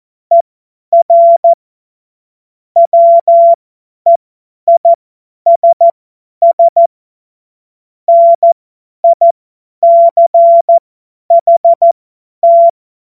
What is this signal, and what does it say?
Morse code, heard as a keyed tone, about 14 wpm, spaced as for 10 wpm, 680 Hz. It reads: ER WEISS NICHT
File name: Morsekurs / Witze / Witz 31. Mai 2022 50 F70